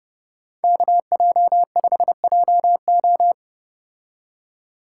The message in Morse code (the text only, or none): KJ5JO